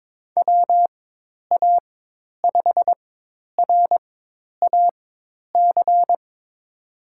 Morse code, heard as a keyed tone, about 22 wpm, spaced as for 10 wpm, 700 Hz. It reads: WA5RAC
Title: USA Call Signs - 6 Characters - 4x Character Spacing - 22wpm